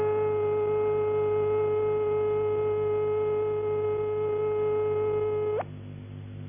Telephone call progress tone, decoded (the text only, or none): dial tone